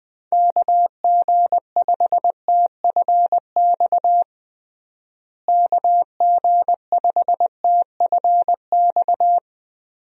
Morse code, heard as a keyed tone, about 21 wpm, 700 Hz. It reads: KG5TFX KG5TFX
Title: USA Call Signs - 6 Characters - Repeated 2x - 20wpm